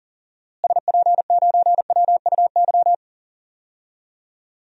Morse code, heard as a keyed tone, about 40 wpm, 700 Hz. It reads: SP9WUY
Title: DXCC-100 Call Signs - 6 Characters - Repeated 2x - 40wpm